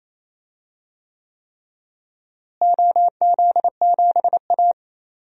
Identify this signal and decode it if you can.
Morse code, heard as a keyed tone, about 28 wpm, 700 Hz. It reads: OZ7A
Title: DXCC-100 Call Signs - 4 Characters - Repeated 3x - 28wpm